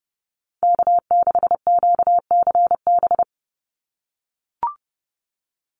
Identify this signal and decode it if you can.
Morse code, heard as a keyed tone, about 30 wpm, 700 Hz. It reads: K6QCB
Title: NA SOTA Chaser Callsigns 30wpm